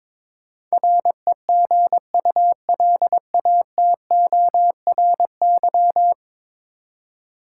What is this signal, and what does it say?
Morse code, heard as a keyed tone, about 21 wpm, 700 Hz. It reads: REGULATORY
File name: practice / 22wpm / Ten Letter Words - 22wpm